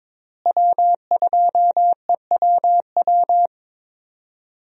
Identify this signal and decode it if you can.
Morse code, heard as a keyed tone, about 22 wpm, 700 Hz. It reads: W2EWW
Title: USA Call Signs - 5 Characters - Repeated 2x - 22wpm